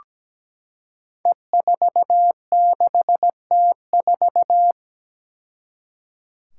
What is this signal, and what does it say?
Morse code, heard as a keyed tone, about 17 wpm, 700 Hz. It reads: E46T4
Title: Five Challenging Letter-Number Combinations 17wpm